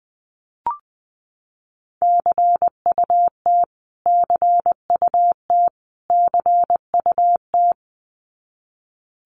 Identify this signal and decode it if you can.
Morse code, heard as a keyed tone, about 20 wpm, 700 Hz. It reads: CUT CUT CUT
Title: Top 500 Words Repeated 20wpm